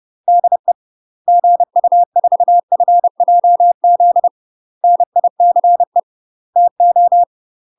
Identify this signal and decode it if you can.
Morse code, heard as a keyed tone, about 30 wpm, 700 Hz. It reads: DE GU4FJZ NICE TO
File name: UK MASTER QSOs TRAINING / 30WPM / UK MASTER QSOs 0018